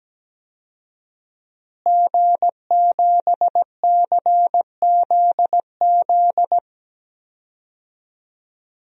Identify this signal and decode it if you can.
Morse code, heard as a keyed tone, about 17 wpm, 700 Hz. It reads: G7CZZ
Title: DXCC-100 Call Signs - 5 Characters - Repeated 2x - 17wpm